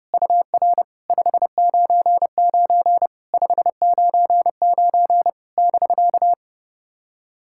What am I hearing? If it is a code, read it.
Morse code, heard as a keyed tone, about 30 wpm, 690 Hz. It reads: UR 599 599 <BK>
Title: SOTA QSO Practice 30wpm